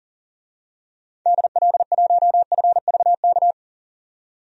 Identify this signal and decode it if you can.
Morse code, heard as a keyed tone, about 40 wpm, 700 Hz. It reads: DL1FVK
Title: DXCC-100 Call Signs - 6 Characters - 40wpm